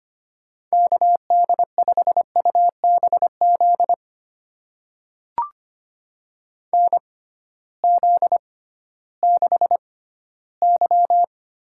Morse code, heard as a keyed tone, about 25 wpm, spaced as for 14 wpm, 700 Hz. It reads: KD5UBZ NZ6Y
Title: USA Call Signs - 6x Character Spacing - 25wpm